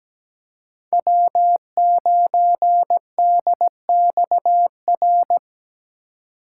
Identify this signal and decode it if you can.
Morse code, heard as a keyed tone, about 17 wpm, 700 Hz. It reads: W9DXR
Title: K1USN SST QSO Practice - 17wpm